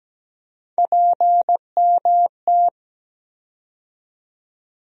Morse code, heard as a keyed tone, about 17 wpm, 700 Hz. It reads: PMT
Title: USA Call Sign Postfixes - 3 Letters 17wpm